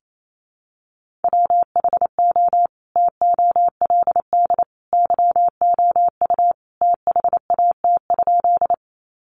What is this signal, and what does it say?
Morse code, heard as a keyed tone, about 28 wpm, 700 Hz. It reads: WHO TOLD YOU THAT?
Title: Sentences from Top 400 Words 28wpm